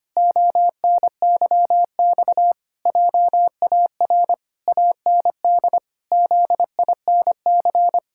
Morse code, heard as a keyed tone, about 25 wpm, 700 Hz. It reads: ONYX JAR AND ZINC
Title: Pangram Challenge v2 - 25wpm